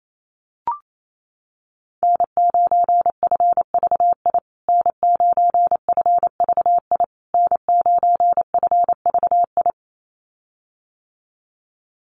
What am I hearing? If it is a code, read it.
Morse code, heard as a keyed tone, about 28 wpm, 700 Hz. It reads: N9FVI N9FVI N9FVI
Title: USA Call Signs Repeated 3x 28wpm